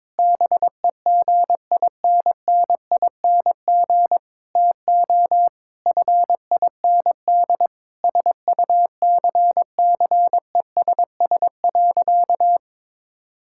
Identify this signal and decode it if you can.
Morse code, heard as a keyed tone, about 21 wpm, 700 Hz. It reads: BEGINNING TO FIND SUCCESS.
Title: Sentences from Top 1000 Words - Review Part 5 - 22wpm